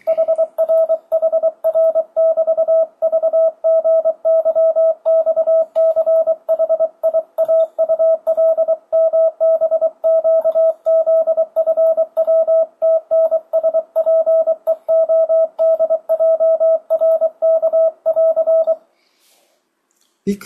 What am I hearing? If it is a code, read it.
Morse code, heard as a keyed tone, about 23 wpm, 650 Hz, with radio noise. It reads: HR HR <BT> VGYXCHIAULMBQZFWTNSPEODJRK <AR>
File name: 2025cw / speedup / 24wpm